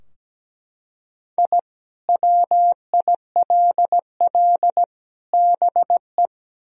Morse code, heard as a keyed tone, about 17 wpm, 700 Hz. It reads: I WILL BE
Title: Sentences from Top 1000 Words - Review Part 3 - 17wpm